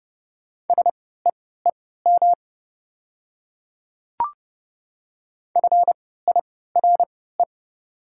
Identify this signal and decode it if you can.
Morse code, heard as a keyed tone, about 30 wpm, spaced as for 17 wpm, 700 Hz. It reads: SEEM FIRE
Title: 3x Character Spacing - Top 500 Words - 30wpm